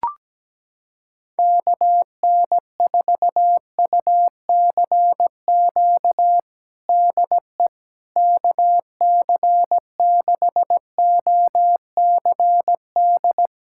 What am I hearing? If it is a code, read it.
Morse code, heard as a keyed tone, about 17 wpm, 700 Hz. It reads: KN4UCQ DE KC6OCD
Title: USA Callsign DE Callsign - 6 Characters - 17wpm